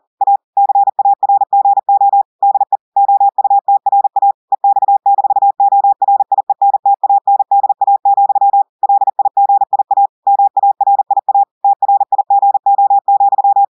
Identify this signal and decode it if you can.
Morse code, heard as a keyed tone, about 40 wpm, 800 Hz. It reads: A CARGO DE OUTRA EX-ORIENTANDA, LIGIA MARIA TRIGO,
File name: audios / portugais / portugais texte en clair 1 vitesse 40 mots minutes